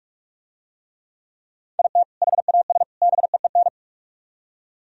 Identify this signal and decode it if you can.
Morse code, heard as a keyed tone, about 45 wpm, 700 Hz. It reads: IT HAS BEEN